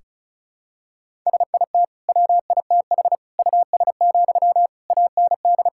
Morse code, heard as a keyed tone, about 36 wpm, 700 Hz. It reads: SIT WITH US, AND
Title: Sentences from Top 1600 Words 35wpm